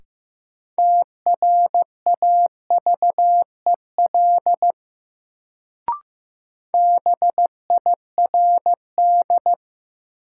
Morse code, heard as a keyed tone, about 15 wpm, 700 Hz. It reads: TRAVEL BIRD
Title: Sentences from Top 400 Words - New Word Drill - 15wpm